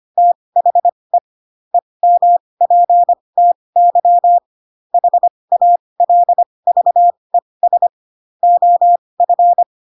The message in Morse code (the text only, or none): THE EMPTY HALVES OF